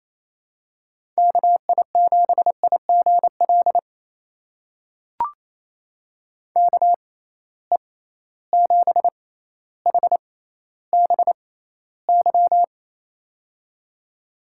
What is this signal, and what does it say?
Morse code, heard as a keyed tone, about 28 wpm, spaced as for 14 wpm, 700 Hz. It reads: KI7IGL KE7HBY
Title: USA Call Signs - 6 Characters - 6x Character Spacing - 28wpm